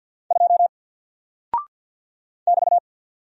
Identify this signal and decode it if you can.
Morse code, heard as a keyed tone, about 48 wpm, 700 Hz. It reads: 2 <BT>
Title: Single Character (Letter:Number:Punctuation:Prosign) Alt 50wpm